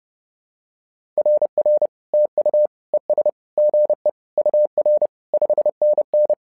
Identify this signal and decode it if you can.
Morse code, heard as a keyed tone, about 30 wpm, 600 Hz. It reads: RR TU ES GE UR 5NN
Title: SOTA QSO Practice 30wpm